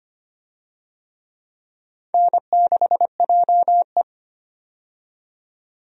Morse code, heard as a keyed tone, about 25 wpm, 700 Hz. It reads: N6JE
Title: USA Call Signs - 4 Characters - 25wpm